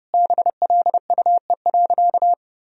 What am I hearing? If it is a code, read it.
Morse code, heard as a keyed tone, about 32 wpm, 700 Hz. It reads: BLUE.
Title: Sentences from Top 1000 Words - Review Part 3 - 30wpm